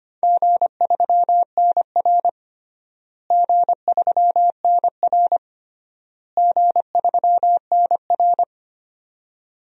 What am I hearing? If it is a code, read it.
Morse code, heard as a keyed tone, about 25 wpm, 700 Hz. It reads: G3NR G3NR G3NR
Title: DXCC-100 Call Signs - 4 Characters - Repeated 3x - 25wpm